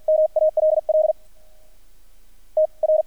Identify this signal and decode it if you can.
Morse code, heard as a keyed tone, about 48 wpm, 620 Hz, with radio noise. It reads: MALL T R
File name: archived / CRSnet